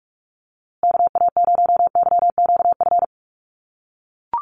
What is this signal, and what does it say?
Morse code, heard as a keyed tone, about 44 wpm, 700 Hz. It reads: KA0YQF
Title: USA Call Signs Repeated 2x 45wpm